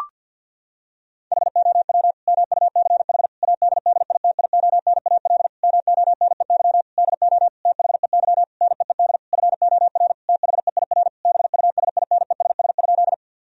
Morse code, heard as a keyed tone, about 50 wpm, 700 Hz. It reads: HOW MUCH ADDITIONAL MONEY DO THEY NEED FOR THEIR BUSINESS?
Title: Sentences from Top 1000 Words - Review Part 4 - 50wpm